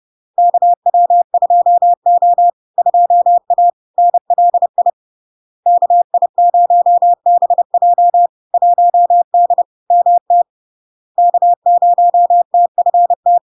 Morse code, heard as a keyed tone, about 30 wpm, 700 Hz. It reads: KW2O 2A NLI KI0BJ 1D MT K0TFT
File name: LARK-FD-2022-Log-in-CW-30wpm-700-Hz